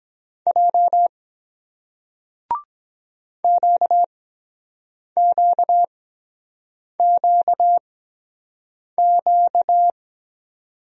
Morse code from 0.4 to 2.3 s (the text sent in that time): J